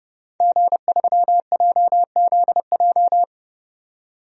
Morse code, heard as a keyed tone, about 32 wpm, 700 Hz. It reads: G3JZJ
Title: DXCC-100 Call Signs - 5 Characters - 30wpm